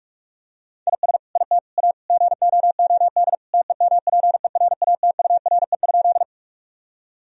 Morse code, heard as a keyed tone, about 47 wpm, 700 Hz. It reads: IS IT A GOOD TEMPERATURE?
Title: Sentences from Top 1000 Words - Review Part 4 - 45wpm